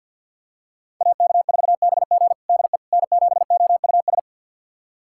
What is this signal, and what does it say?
Morse code, heard as a keyed tone, about 51 wpm, 700 Hz. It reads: AK4BG DE N7OUS
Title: USA Callsign DE Callsign - 5 Characters - 50wpm